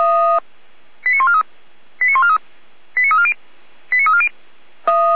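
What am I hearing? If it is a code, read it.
Five-tone selective call (ZVEI), heard as quick runs of five tones, tones of about 70 ms. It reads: 89143 89143 89340 89340
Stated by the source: ZVEI five-tone selective call received via SDR, code 89280.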